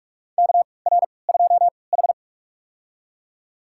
Morse code, heard as a keyed tone, about 46 wpm, spaced as for 25 wpm, 700 Hz. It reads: KR2H
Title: USA Call Signs - 4 Characters - 3x Character Spacing - 45wpm